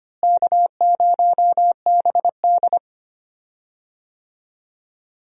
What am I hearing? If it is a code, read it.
Morse code, heard as a keyed tone, about 25 wpm, 700 Hz. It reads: K0BD
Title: USA Call Signs - 4 Characters - Repeated 2x - 25wpm